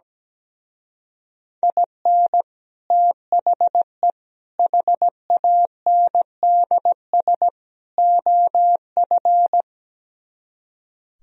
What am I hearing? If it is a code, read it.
Morse code, heard as a keyed tone, about 17 wpm, 700 Hz. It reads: IN THE HANDS OF
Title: Speed-Racing - Sets of 4 Words 17wpm